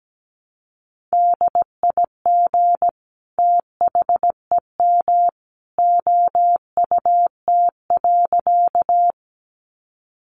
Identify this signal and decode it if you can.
Morse code, heard as a keyed tone, about 17 wpm, 700 Hz. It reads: DIG THEM OUT.
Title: A Sound Of Thunder 17wpm